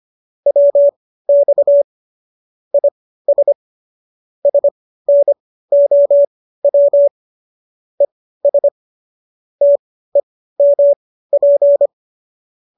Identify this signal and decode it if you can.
Morse code, heard as a keyed tone, about 25 wpm, spaced as for 15 wpm, 570 Hz. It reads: WX IS SNOW ES TEMP